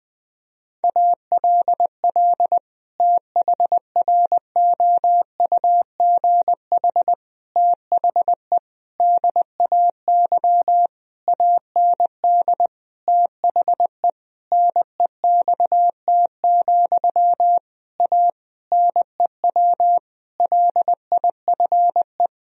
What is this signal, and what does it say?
Morse code, heard as a keyed tone, about 20 wpm, 700 Hz. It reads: ALL THROUGH THE DAY AND THE NEXT, A NEW LIFE
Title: Sentences from Top 1000 Words - Review Part 1 - 20wpm